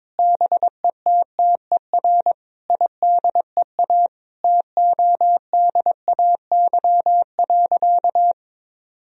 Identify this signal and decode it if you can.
Morse code, heard as a keyed tone, about 21 wpm, 700 Hz. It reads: BETTER IDEA TODAY.
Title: Sentences from Top 1000 Words - Review Part 4 - 22wpm